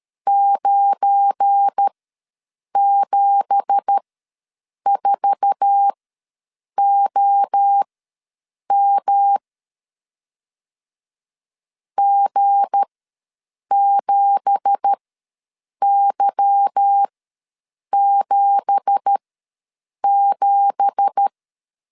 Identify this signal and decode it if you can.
Morse code, heard as a keyed tone, about 13 wpm, spaced as for 7 wpm, 800 Hz. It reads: 974OM G7Y77
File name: LEZIONI / 9 2867 ws 12wpm 3431